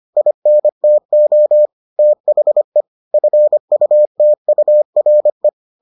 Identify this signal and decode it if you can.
Morse code, heard as a keyed tone, about 25 wpm, 600 Hz. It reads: INTO THE FUTURE